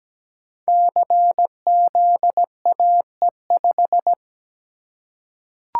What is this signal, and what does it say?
Morse code, heard as a keyed tone, about 17 wpm, 700 Hz. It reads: CZAE5
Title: Code Groups - 5 Letter:Numbers - 17wpm